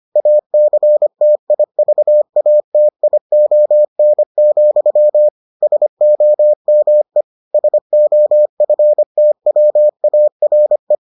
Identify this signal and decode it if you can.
Morse code, heard as a keyed tone, about 25 wpm, 600 Hz. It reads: ACTIVATION, SOME SOFTWARE